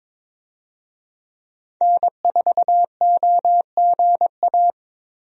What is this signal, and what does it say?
Morse code, heard as a keyed tone, about 22 wpm, 700 Hz. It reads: N4OGA
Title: USA Call Signs - 5 Characters - 22wpm